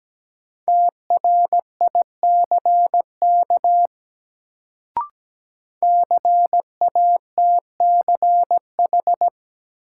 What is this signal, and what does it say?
Morse code, heard as a keyed tone, about 17 wpm, 700 Hz. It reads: TRICK CATCH
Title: Sentences from Top 1500 Words - New Word Drill Alt - 17wpm